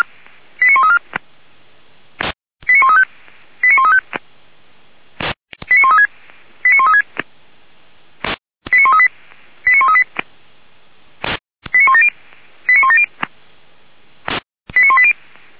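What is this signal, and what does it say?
Five-tone selective call (ZVEI), heard as quick runs of five tones, tones of about 80 ms. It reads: 89134 89135 89135 89136 89136 89138 89138 89170 89170 89180
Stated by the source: ZVEI five-tone call 89133 received by SDR, archived by Feuerwehr Orlamünde fire brigade.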